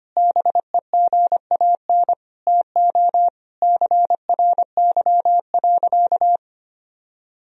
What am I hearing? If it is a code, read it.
Morse code, heard as a keyed tone, about 25 wpm, 700 Hz. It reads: BEGAN TO CRY.
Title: Sentences from Top 1000 Words - Review Part 2 - 25wpm